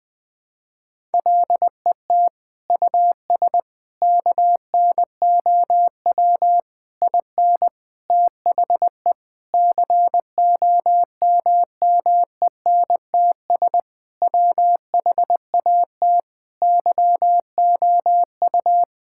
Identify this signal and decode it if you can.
Morse code, heard as a keyed tone, about 19 wpm, 700 Hz. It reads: LET US KNOW IN THE COMMENTS WHAT YOU
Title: Sentences with 6-Word Phrases - 20wpm